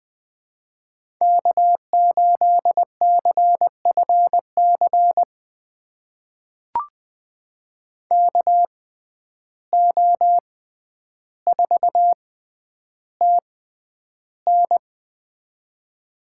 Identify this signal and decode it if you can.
Morse code, heard as a keyed tone, about 20 wpm, spaced as for 10 wpm, 700 Hz. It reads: K8CFC KO4TN
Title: USA Call Signs - 5 Characters - 6x Character Spacing - 20wpm